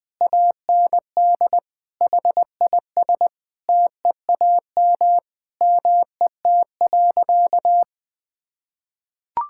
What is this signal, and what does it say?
Morse code, heard as a keyed tone, about 20 wpm, 700 Hz. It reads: AND HIS TEAM MET.
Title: Sentences with 5-Word Phrases - 20wpm